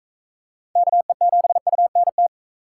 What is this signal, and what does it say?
Morse code, heard as a keyed tone, about 45 wpm, 700 Hz. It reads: KE7UNT